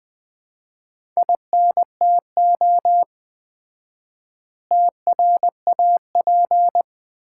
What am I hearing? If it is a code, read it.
Morse code, heard as a keyed tone, about 20 wpm, 700 Hz. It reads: INTO TRAP
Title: Common QSO Phrases - 4x Word-Spacing - 20wpm